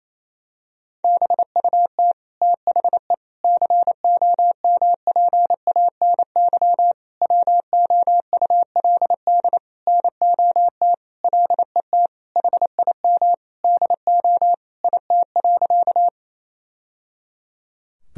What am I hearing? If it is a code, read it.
Morse code, heard as a keyed tone, about 27 wpm, 700 Hz. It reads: BUT THE COMPANY WOULD NOT LET HIM DO IT.
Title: Sentences from Top 1000 Words - Review Part 3 - 28wpm